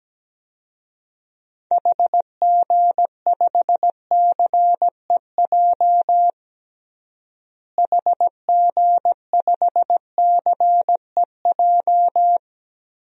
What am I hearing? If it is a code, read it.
Morse code, heard as a keyed tone, about 17 wpm, 700 Hz. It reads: HG5CEJ HG5CEJ
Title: DXCC-100 Call Signs - 6 Characters - Repeated 3x - 17wpm